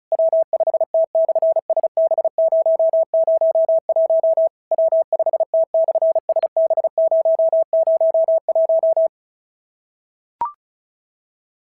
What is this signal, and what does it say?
Morse code, heard as a keyed tone, about 34 wpm, 650 Hz. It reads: W5T/SB001 W5T/SB001
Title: SOTA QSO Practice 35wpm